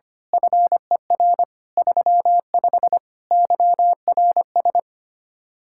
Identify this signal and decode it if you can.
Morse code, heard as a keyed tone, about 24 wpm, 700 Hz. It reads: FER 35 YRS
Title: Speed-Racing Common QSO Phrases - 25wpm